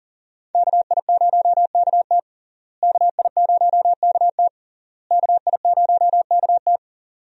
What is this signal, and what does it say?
Morse code, heard as a keyed tone, about 40 wpm, 700 Hz. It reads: KI0KT KI0KT KI0KT
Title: USA Call Signs - 5 Characters - Repeated 3x - 40wpm